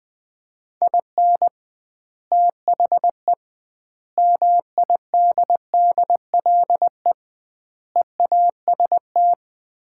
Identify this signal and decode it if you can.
Morse code, heard as a keyed tone, about 20 wpm, 700 Hz. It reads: IN THE MIDDLE EAST
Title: 2x Word-Spacing - Sets of 4 Words - 20wpm